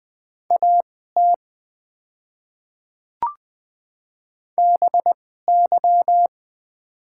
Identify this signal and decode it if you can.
Morse code, heard as a keyed tone, about 20 wpm, spaced as for 14 wpm, 700 Hz. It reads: AT BY
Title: 2x Character Spacing - Top 100 Words - 20wpm